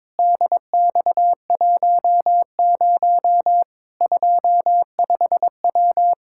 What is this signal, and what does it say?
Morse code, heard as a keyed tone, about 22 wpm, 700 Hz. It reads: DX10 25W
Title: Practice QSO - Exchange 2 - Info - 22wpm